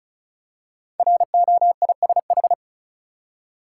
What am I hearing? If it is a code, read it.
Morse code, heard as a keyed tone, about 36 wpm, 700 Hz. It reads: ROISH
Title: Code Groups - 5 Letters - 35wpm